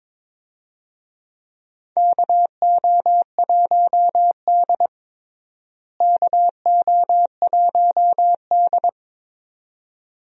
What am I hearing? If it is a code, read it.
Morse code, heard as a keyed tone, about 22 wpm, 700 Hz. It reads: KO1D KO1D
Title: USA Call Signs - 4 Characters - Repeated 2x - 22wpm